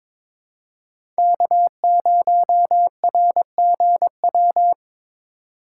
Morse code, heard as a keyed tone, about 22 wpm, 700 Hz. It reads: K0RGW
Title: USA Call Signs - 3x Character Spacing - 22wpm